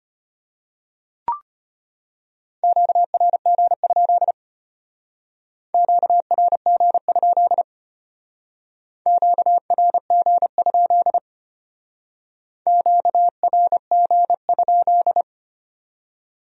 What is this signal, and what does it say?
Morse code, heard as a keyed tone, about 32 wpm, 700 Hz. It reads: QRG? QRG? QRG? QRG?